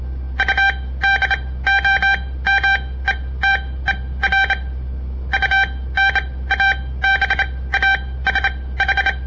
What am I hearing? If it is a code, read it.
Morse code, heard as a keyed tone, about 27 wpm, spaced as for 18 wpm, 1550 Hz. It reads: UDOMETER UNABASH
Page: On the air Morse code practice from the Southern Montana Amatuer Radio Association; all code, no read back